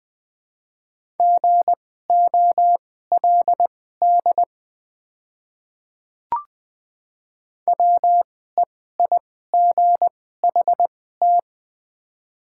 Morse code, heard as a keyed tone, about 20 wpm, spaced as for 15 wpm, 690 Hz. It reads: GOLD WEIGHT
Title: 2x Character Spacing - Top 500 Words - 20wpm